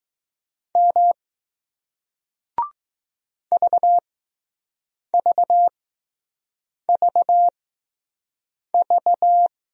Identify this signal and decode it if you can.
Morse code, heard as a keyed tone, about 20 wpm, 700 Hz. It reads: M V V V V